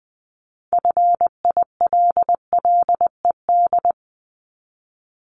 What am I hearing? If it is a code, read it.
Morse code, heard as a keyed tone, about 19 wpm, 700 Hz. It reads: FILLED
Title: Multi-letter Phonemes Repeated 3x 20wpm